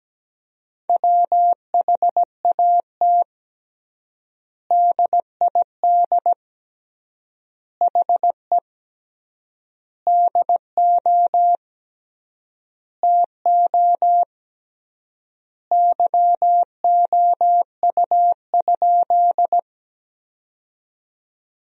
Morse code, heard as a keyed tone, about 17 wpm, 700 Hz. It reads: WHAT DID HE DO TO YOU?
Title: 3x Word-Spacing - Sentences from Top 100 Words - 17wpm